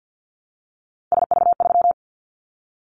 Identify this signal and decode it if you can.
Morse code, heard as a keyed tone, about 50 wpm, 700 Hz. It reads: SV3
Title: Callsign Prefixes for Top 50 DXCC - 50wpm